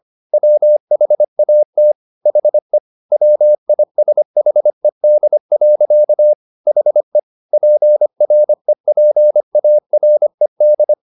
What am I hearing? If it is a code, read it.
Morse code, heard as a keyed tone, about 25 wpm, 600 Hz. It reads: WHAT HE WISHED. HE PREPARED